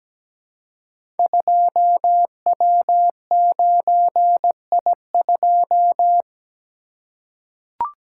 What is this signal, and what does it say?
Morse code, heard as a keyed tone, about 17 wpm, 700 Hz. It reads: 2W9I2
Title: Code Groups - 5 Letter:Numbers - 17wpm